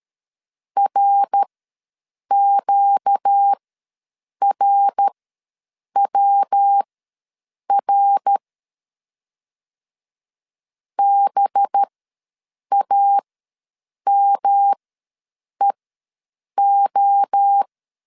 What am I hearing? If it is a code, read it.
Morse code, heard as a keyed tone, about 13 wpm, spaced as for 7 wpm, 800 Hz. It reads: RQRWR BAMEO
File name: LEZIONI / 5 LBVC / 5 LB ws 12wpm 3034